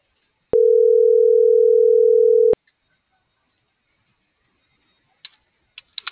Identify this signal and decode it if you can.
Ringback tone (North America), 440 + 480 Hz.